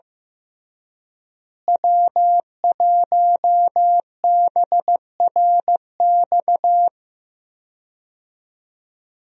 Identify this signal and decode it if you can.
Morse code, heard as a keyed tone, about 15 wpm, 700 Hz. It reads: W1BRX
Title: USA Call Signs - 5 Characters - Repeated 3x - 15wpm